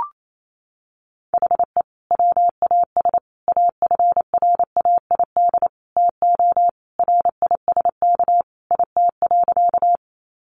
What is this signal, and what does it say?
Morse code, heard as a keyed tone, about 28 wpm, 700 Hz. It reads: HE WAS AFRAID TO RISK IT.
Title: Sentences from Top 1500 Words 28wpm